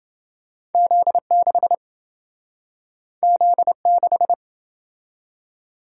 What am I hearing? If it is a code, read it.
Morse code, heard as a keyed tone, about 28 wpm, 700 Hz. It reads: Z6 Z6